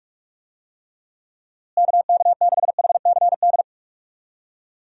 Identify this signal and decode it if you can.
Morse code, heard as a keyed tone, about 46 wpm, 700 Hz. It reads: KK6HCD
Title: USA Call Signs - 6 Characters - Repeated 3x - 45wpm